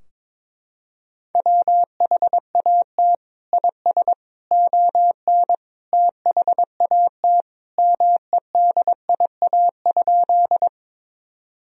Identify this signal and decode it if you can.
Morse code, heard as a keyed tone, about 22 wpm, 700 Hz. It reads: WHAT IS ON THAT MEDIA?
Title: Sentences from Top 1000 Words - Review Part 4 - 22wpm